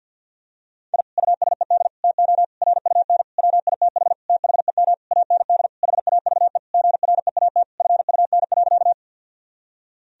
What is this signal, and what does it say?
Morse code, heard as a keyed tone, about 51 wpm, 700 Hz. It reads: I USED TO RUN WITH THEM AND HAVE GREAT FUN.
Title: Sentences from Top 1000 Words - Review Part 4 - 50wpm